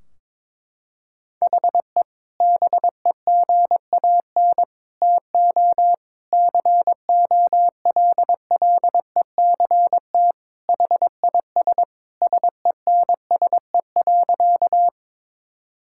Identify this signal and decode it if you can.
Morse code, heard as a keyed tone, about 22 wpm, 700 Hz. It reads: HE BEGAN TO COLLECT HIS SENSE.
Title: Sentences from Top 1600 Words 22wpm